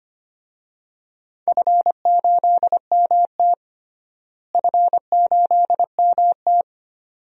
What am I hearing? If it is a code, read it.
Morse code, heard as a keyed tone, about 25 wpm, 700 Hz. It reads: F8MT F8MT
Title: DXCC-100 Call Signs - 4 Characters - Repeated 2x - 25wpm